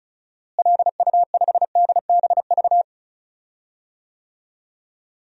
Morse code, heard as a keyed tone, about 34 wpm, 700 Hz. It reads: LU5DBV
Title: DXCC-100 Call Signs - 6 Characters - Repeated 3x - 35wpm